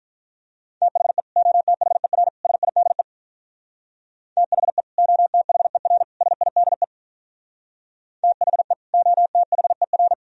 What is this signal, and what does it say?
Morse code, heard as a keyed tone, about 43 wpm, 700 Hz. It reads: THE OTHER SIDE THE OTHER SIDE THE OTHER
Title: Speed-Racing - Sets of 3 Words 35wpm